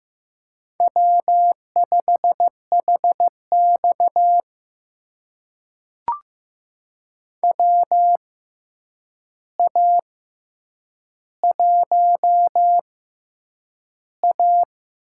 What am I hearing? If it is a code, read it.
Morse code, heard as a keyed tone, about 15 wpm, spaced as for 8 wpm, 700 Hz. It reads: W5HX WA1A
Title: USA Call Signs - 4 Characters - 6x Character Spacing - 15wpm